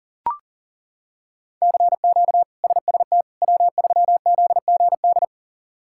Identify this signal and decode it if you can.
Morse code, heard as a keyed tone, about 40 wpm, 700 Hz. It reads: CQ SST W3ZGD
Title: K1USN SST QSO Practice - 40wpm